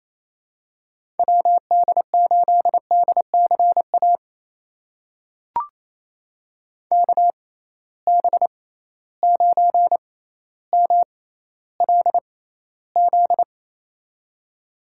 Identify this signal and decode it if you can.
Morse code, heard as a keyed tone, about 28 wpm, spaced as for 14 wpm, 700 Hz. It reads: WD8DCA KB9MLZ
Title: USA Call Signs - 6 Characters - 6x Character Spacing - 28wpm